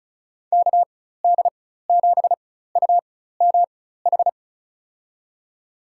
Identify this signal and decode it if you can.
Morse code, heard as a keyed tone, about 34 wpm, spaced as for 16 wpm, 700 Hz. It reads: KD7UMH
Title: USA Call Signs - 6 Characters - 4x Character Spacing - 35wpm